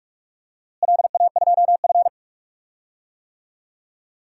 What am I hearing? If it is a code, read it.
Morse code, heard as a keyed tone, about 46 wpm, 700 Hz. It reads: LA2F
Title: DXCC-100 Call Signs - 4 Characters - Repeated 2x - 45wpm